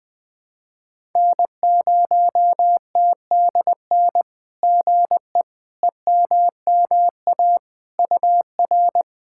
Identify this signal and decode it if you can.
Morse code, heard as a keyed tone, about 20 wpm, 700 Hz. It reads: N0TDN GE EMMA UR
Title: SOTA QSO Practice 20wpm